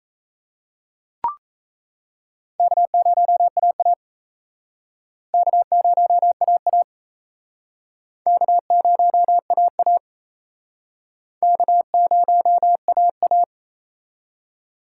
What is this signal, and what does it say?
Morse code, heard as a keyed tone, about 36 wpm, 700 Hz. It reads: K0AA K0AA K0AA K0AA